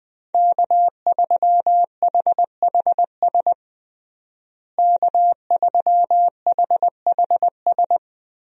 Morse code, heard as a keyed tone, about 19 wpm, 700 Hz. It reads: K3HHS K3HHS
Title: USA Call Signs - 5 Characters - Repeated 3x - 20wpm